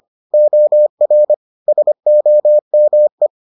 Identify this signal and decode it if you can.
Morse code, heard as a keyed tone, about 25 wpm, 600 Hz. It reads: OR SOME